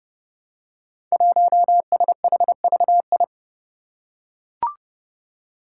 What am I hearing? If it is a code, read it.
Morse code, heard as a keyed tone, about 30 wpm, 700 Hz. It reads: 1SHVI
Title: Five Challenging Letter-Number Combinations 30wpm